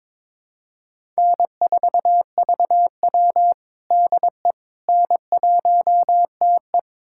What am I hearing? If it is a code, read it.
Morse code, heard as a keyed tone, about 22 wpm, 700 Hz. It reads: N4VW DE N1TE
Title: USA Callsign DE Callsign - 4 Characters - 22wpm